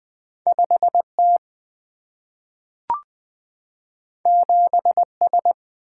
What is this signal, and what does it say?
Morse code, heard as a keyed tone, about 20 wpm, 700 Hz. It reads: 5T 7S
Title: Single-Letter Single-Number 20wpm